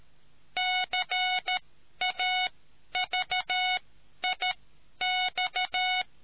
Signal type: Morse code